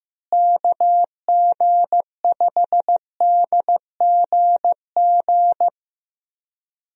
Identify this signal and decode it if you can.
Morse code, heard as a keyed tone, about 15 wpm, 700 Hz. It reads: KG5DGG
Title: USA Call Signs - 6 Characters - Repeated 3x - 15wpm